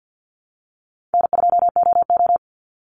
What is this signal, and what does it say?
Morse code, heard as a keyed tone, about 50 wpm, 700 Hz. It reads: N2OO
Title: Call Signs - 50wpm